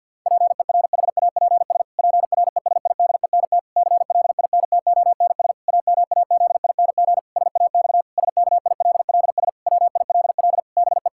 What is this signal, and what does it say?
Morse code, heard as a keyed tone, about 50 wpm, 700 Hz. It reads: PERHAPS PRESIDENT CLINTONS AMAZING SAX SKILLS WILL BE
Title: Pangram Challenge v2 - 50wpm